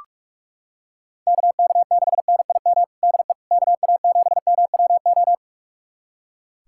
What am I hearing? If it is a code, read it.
Morse code, heard as a keyed tone, about 41 wpm, 700 Hz. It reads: KK6NIM DE KA7MWO
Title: USA Callsign DE Callsign - 6 Characters - 45wpm